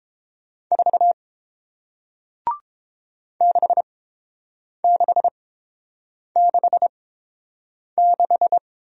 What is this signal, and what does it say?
Morse code, heard as a keyed tone, about 30 wpm, spaced as for 7 wpm, 700 Hz. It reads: 4 6666